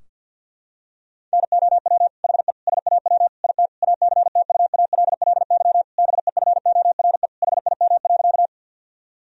Morse code, heard as a keyed tone, about 51 wpm, 700 Hz. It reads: NOW HE SAW IT ACTUALLY BEFORE HIM.
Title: Sentences from Top 1000 Words - Review Part 4 - 50wpm